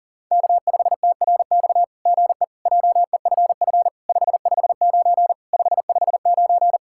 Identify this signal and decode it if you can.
Morse code, heard as a keyed tone, about 38 wpm, 690 Hz. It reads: K5TRX GE JEFF 559 559
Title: SOTA QSO Practice 40wpm